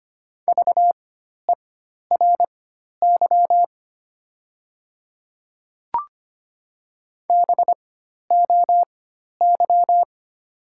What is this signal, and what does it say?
Morse code, heard as a keyed tone, about 25 wpm, spaced as for 12 wpm, 700 Hz. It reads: VERY BOY
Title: 4x Character Spacing - Top 200 Words - 25wpm